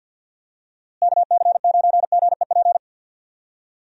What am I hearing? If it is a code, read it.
Morse code, heard as a keyed tone, about 47 wpm, 700 Hz. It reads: KC9GEP
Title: USA Call Signs - 6 Characters - Repeated 2x - 50wpm